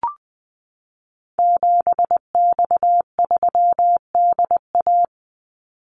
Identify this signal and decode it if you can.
Morse code, heard as a keyed tone, about 20 wpm, 700 Hz. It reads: 7X3DA
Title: Call Signs Repeated 3x - 20wpm